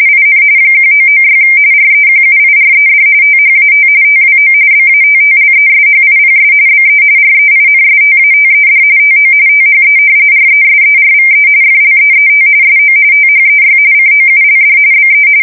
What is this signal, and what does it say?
RTTY, about 45.45 baud. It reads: IN 1996 REQUIRED THAT TWO BATTLESHIPS BE REINSTATED TO THE RESERVE FLEET SO THEY COULD BE
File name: fsk170 / battleshipnj